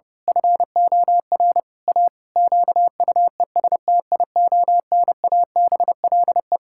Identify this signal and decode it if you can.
Morse code, heard as a keyed tone, about 30 wpm, 700 Hz. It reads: FOR A QUESTIONABLE
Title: Pangram Challenge v2 - 30wpm